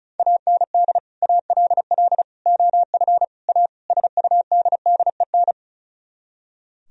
Morse code, heard as a keyed tone, about 43 wpm, 700 Hz. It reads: AND ALL OF A SUDDEN
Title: Speed-Racing - Sets of 5 Words 35wpm